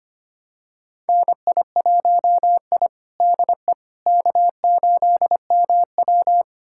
Morse code, heard as a keyed tone, about 25 wpm, 700 Hz. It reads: NI1I DE K8MW
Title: USA Callsign DE Callsign - 4 Characters - 25wpm